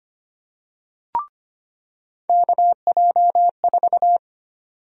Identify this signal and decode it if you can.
Morse code, heard as a keyed tone, about 25 wpm, 700 Hz. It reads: KJ4
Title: USA Call Sign Prefixes - 25wpm